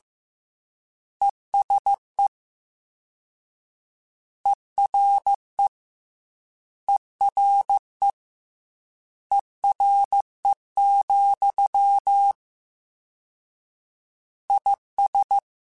Morse code, heard as a keyed tone, about 15 wpm, 800 Hz. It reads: ESE ERE ERE ERE, IS
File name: Combinations / Combinations-mp3s / Combinations@13